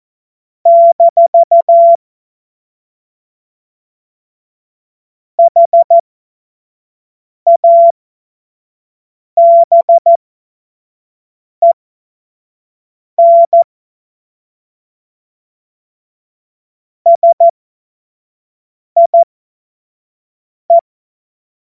Morse code, heard as a keyed tone, about 14 wpm, spaced as for 5 wpm, 680 Hz. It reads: - HABEN SIE
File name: Morsekurs / Witze / Witz 13 Februar 2024 25 F70